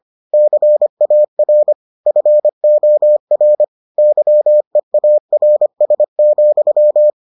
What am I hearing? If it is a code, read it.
Morse code, heard as a keyed tone, about 25 wpm, 600 Hz. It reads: CAR FOR YEARS,